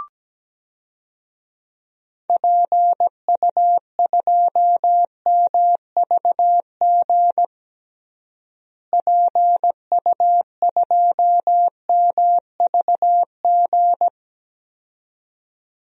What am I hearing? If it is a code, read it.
Morse code, heard as a keyed tone, about 17 wpm, 700 Hz. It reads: PU2MVG PU2MVG
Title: DXCC-100 Call Signs - 6 Characters - Repeated 2x - 17wpm